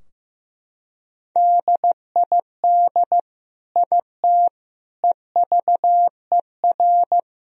Morse code, heard as a keyed tone, about 16 wpm, 700 Hz. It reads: DID IT EVER
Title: Sentences from Top 1000 Words - Review Part 2 - 15wpm